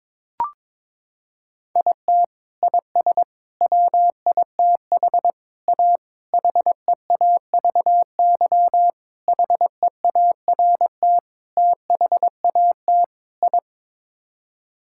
Morse code, heard as a keyed tone, about 21 wpm, 700 Hz. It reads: IT IS WITH A HEAVY HEART THAT I
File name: practice / 22wpm / Sets of 8 Words - 22wpm